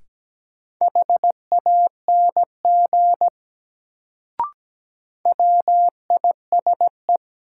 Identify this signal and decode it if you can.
Morse code, heard as a keyed tone, about 17 wpm, 700 Hz. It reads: HANG WISE
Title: Sentences from Top 1600 Words - New Word Drill - 17wpm